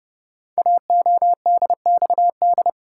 Morse code, heard as a keyed tone, about 30 wpm, 690 Hz. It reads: AODXD
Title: Code Groups - 5 Letters - 30wpm